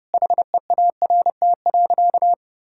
Morse code, heard as a keyed tone, about 32 wpm, 700 Hz. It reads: HEART.
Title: Sentences from Top 1600 Words 30wpm